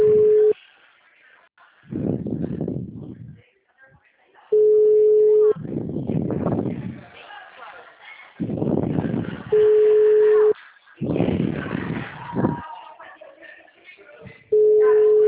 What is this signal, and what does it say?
Ringback tone (Europe, 425 Hz), with speech.
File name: out- -2015- - - .2